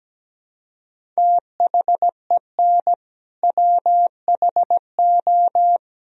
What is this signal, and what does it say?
Morse code, heard as a keyed tone, about 17 wpm, 700 Hz. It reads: THEN WHO
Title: Sentences from Top 100 Words and Abbreviations 17wpm